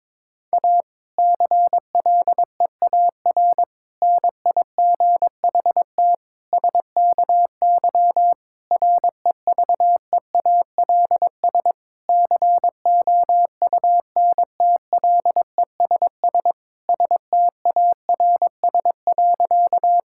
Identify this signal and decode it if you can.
Morse code, heard as a keyed tone, about 22 wpm, 700 Hz. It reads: A CLEAR NIGHT SKY REVEALS COUNTLESS STARS.
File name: Difficult Sentences with Hints - 22wpm